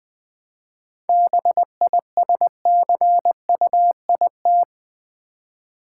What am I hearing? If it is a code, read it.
Morse code, heard as a keyed tone, about 20 wpm, 700 Hz. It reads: BISCUIT
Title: 7 Letter Words - First 3 Letters Given - 20wpm